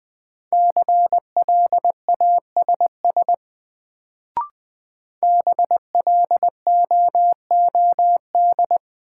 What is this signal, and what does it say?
Morse code, heard as a keyed tone, about 20 wpm, 700 Hz. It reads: CLASS BLOOD
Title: Sentences from Top 1000 Words - Review Part 2 - New Word Drill - 20wpm